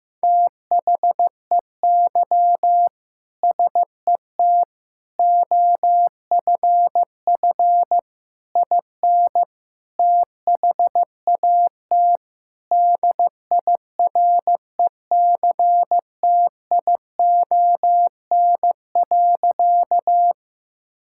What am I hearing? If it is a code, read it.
Morse code, heard as a keyed tone, about 15 wpm, 700 Hz. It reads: THEY SET OFF IN THAT DIRECTION.
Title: Sentences from Top 1000 Words - Review Part 5 - 15wpm